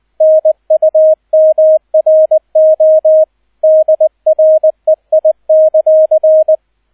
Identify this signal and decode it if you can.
Morse code, heard as a keyed tone, about 20 wpm, 630 Hz, with radio noise. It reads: NUMRO DREI;